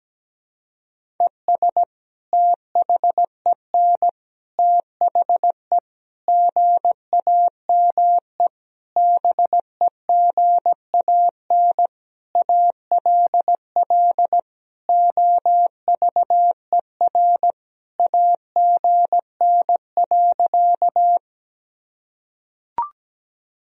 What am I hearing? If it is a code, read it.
Morse code, heard as a keyed tone, about 17 wpm, 700 Hz. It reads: ES THEN THE GAME BEGAN ALL OVER AGN.
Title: Sentences from Top 500 Words and Abbreviations 17wpm